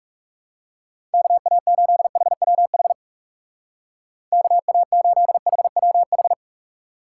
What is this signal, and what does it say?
Morse code, heard as a keyed tone, about 37 wpm, 700 Hz. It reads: KA8HWH KA8HWH
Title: Speed Racing - USA 2x3 Call Signs 30wpm